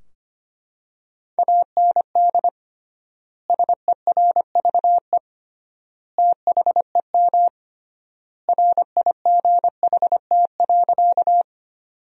Morse code, heard as a keyed tone, about 25 wpm, 700 Hz. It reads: AND SERVE THEM RIGHT.
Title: 3x Word-Spacing - Sentences from Top 400 Words - 25wpm